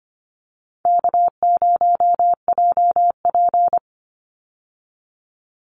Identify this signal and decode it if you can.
Morse code, heard as a keyed tone, about 24 wpm, 700 Hz. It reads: K0JP
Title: Easy Call Signs - 25wpm